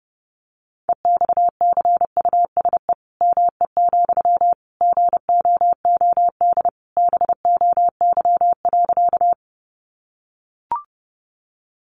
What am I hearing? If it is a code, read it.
Morse code, heard as a keyed tone, about 30 wpm, 700 Hz. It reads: EXCUSE ME, GOOD BOY.
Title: Sentences from Top 700 Words 30wpm